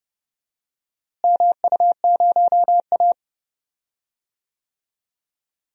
Morse code, heard as a keyed tone, about 29 wpm, 700 Hz. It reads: MU0A
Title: DXCC-100 Call Signs - 4 Characters - Repeated 2x - 30wpm